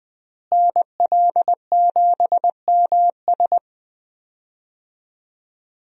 Morse code, heard as a keyed tone, about 20 wpm, 700 Hz. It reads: NL7MS
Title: USA Call Signs - 5 Characters - Repeated 3x - 20wpm